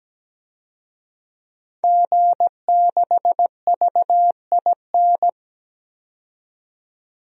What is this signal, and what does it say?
Morse code, heard as a keyed tone, about 17 wpm, 700 Hz. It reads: G6VIN
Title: DXCC-100 Call Signs - 5 Characters - 17wpm